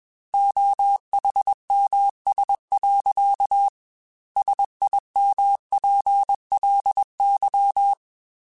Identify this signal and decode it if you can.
Morse code, heard as a keyed tone, about 21 wpm, 800 Hz. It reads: OHMS. SIMPLY
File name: Technical Subjects / Technical Subjects-mp3s / XC @20